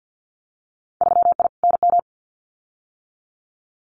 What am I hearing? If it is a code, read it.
Morse code, heard as a keyed tone, about 50 wpm, 700 Hz. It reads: 3I NM
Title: Winter Field Day Exchanges Encore 50wpm